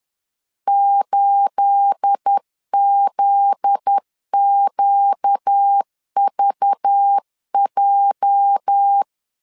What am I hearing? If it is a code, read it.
Morse code, heard as a keyed tone, about 10 wpm, 800 Hz. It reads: 8ZQVJ
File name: CW / DET / 1 9 1500C S A 10wpm 4511